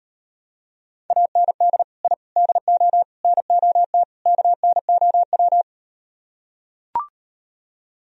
Morse code, heard as a keyed tone, about 38 wpm, 700 Hz. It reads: AND I DO NOT KNOW